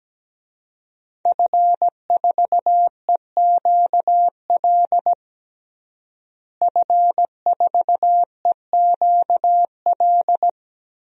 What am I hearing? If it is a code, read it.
Morse code, heard as a keyed tone, about 17 wpm, 700 Hz. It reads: F4EQL F4EQL
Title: DXCC-100 Call Signs - 5 Characters - Repeated 3x - 17wpm